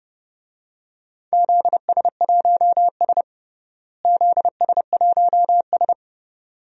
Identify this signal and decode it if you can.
Morse code, heard as a keyed tone, about 30 wpm, 700 Hz. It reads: ZS1S ZS1S
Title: DXCC-100 Call Signs - 4 Characters - Repeated 2x - 30wpm